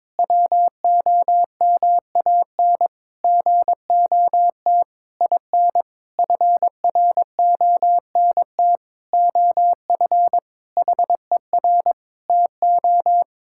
Morse code, heard as a keyed tone, about 22 wpm, 700 Hz. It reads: WOMAN GOT IN FRONT OF HER TO
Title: Sentences from Top 1600 Words 22wpm